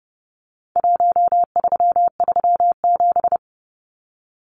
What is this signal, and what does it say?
Morse code, heard as a keyed tone, about 30 wpm, 700 Hz. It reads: 1337
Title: Numbers - 4-Digit 30wpm